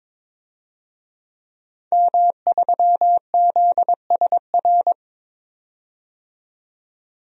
Morse code, heard as a keyed tone, about 23 wpm, 700 Hz. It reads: M3ZSR
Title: DXCC-100 Call Signs - 5 Characters - 22wpm